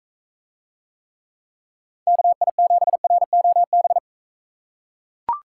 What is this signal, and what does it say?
Morse code, heard as a keyed tone, about 43 wpm, 700 Hz. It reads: KI7ROB